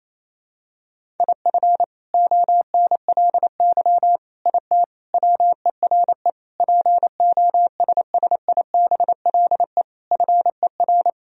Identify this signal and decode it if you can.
Morse code, heard as a keyed tone, about 28 wpm, 700 Hz. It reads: IF ONLY IT WERE POSSIBLE FER
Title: Sentences from Top 500 Words and Abbreviations 28wpm